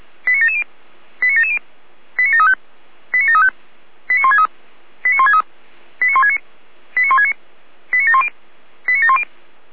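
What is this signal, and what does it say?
Five-tone selective call (ZVEI), heard as quick runs of five tones, tones of about 70 ms. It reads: 89770 89770 89735 89735 89163 89163 89169 89169 89710 89710
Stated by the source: ZVEI five-tone call 89584 received on an SDR, channel 455.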